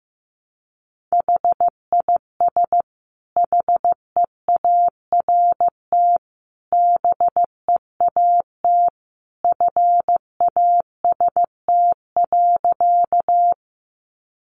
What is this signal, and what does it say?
Morse code, heard as a keyed tone, about 14 wpm, 700 Hz. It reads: HIS HEART BEAT FAST.
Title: Sentences from Top 1300 Words 15wpm